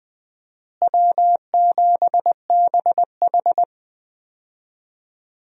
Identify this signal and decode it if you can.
Morse code, heard as a keyed tone, about 20 wpm, 700 Hz. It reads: W7BH
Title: USA Call Signs - 4 Characters - Repeated 2x - 20wpm